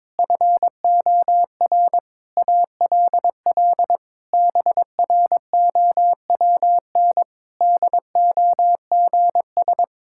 Morse code, heard as a keyed tone, about 22 wpm, 700 Hz. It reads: FOR ALL BROWN DOGS
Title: Pangram Challenge v2 - 22wpm